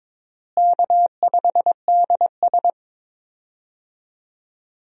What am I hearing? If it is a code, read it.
Morse code, heard as a keyed tone, about 22 wpm, 700 Hz. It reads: K5DS
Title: USA Call Signs - 4 Characters - Repeated 3x - 22wpm